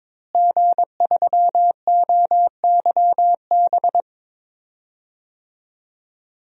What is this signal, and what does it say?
Morse code, heard as a keyed tone, about 22 wpm, 700 Hz. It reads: G3OYB
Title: DXCC-100 Call Signs - 5 Characters - Repeated 2x - 22wpm